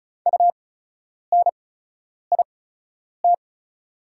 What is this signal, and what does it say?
Morse code, heard as a keyed tone, about 36 wpm, spaced as for 10 wpm, 700 Hz. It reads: UNIT
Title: 8x Character Spacing - Top 500 Words - 35wpm